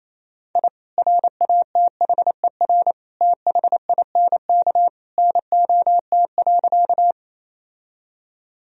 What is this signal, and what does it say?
Morse code, heard as a keyed tone, about 27 wpm, 700 Hz. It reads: I RATHER THINK NOT.
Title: Sentences from Top 1000 Words - Review Part 4 - 28wpm